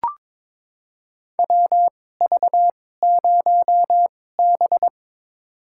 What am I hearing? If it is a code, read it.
Morse code, heard as a keyed tone, about 22 wpm, spaced as for 16 wpm, 700 Hz. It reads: WV0B
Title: USA Call Signs - 4 Characters - 2x Character Spacing - 22wpm